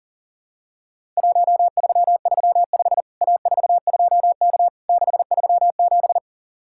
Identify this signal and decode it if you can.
Morse code, heard as a keyed tone, about 40 wpm, 700 Hz. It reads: 1335 A42K 637
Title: Worked All Europe DX Contest - QTC Exchange - 40wpm